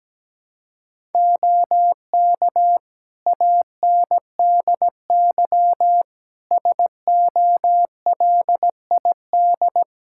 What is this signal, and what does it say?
Morse code, heard as a keyed tone, about 17 wpm, 700 Hz. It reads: OK ANDY SOLID
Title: K1USN SST QSO Practice 17wpm